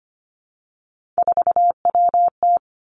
Morse code, heard as a keyed tone, about 25 wpm, 700 Hz. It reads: 4WT
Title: Three Challenging Letter-Number Combinations 25wpm